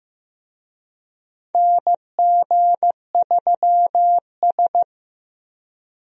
Morse code, heard as a keyed tone, about 15 wpm, 700 Hz. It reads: NG3S
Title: USA Call Signs - 4 Characters - 15wpm